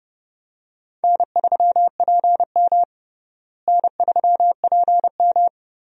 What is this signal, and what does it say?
Morse code, heard as a keyed tone, about 30 wpm, 700 Hz. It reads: N3PM N3PM
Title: USA Call Signs - 4 Characters - Repeated 2x - 30wpm